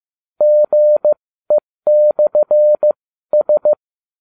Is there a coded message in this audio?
Yes: Morse code from 0.4 to 3.7 s, heard as a keyed tone, about 15 wpm, spaced as for 13 wpm, 600 Hz. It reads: GE/S